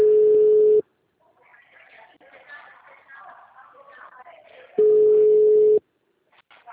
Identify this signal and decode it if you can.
Ringback tone (Europe, 425 Hz).